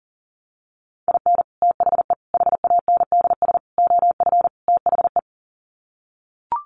Morse code, heard as a keyed tone, about 40 wpm, 700 Hz. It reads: IN THE HANDS OF THE
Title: Sets of 5 Words 40wpm